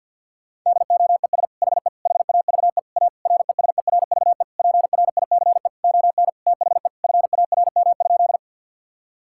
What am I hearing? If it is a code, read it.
Morse code, heard as a keyed tone, about 50 wpm, 700 Hz. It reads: DOES HE HAVE A RESERVE PRICE ON THE FARM?
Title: Sentences from Top 1000 Words - Review Part 6 - 50wpm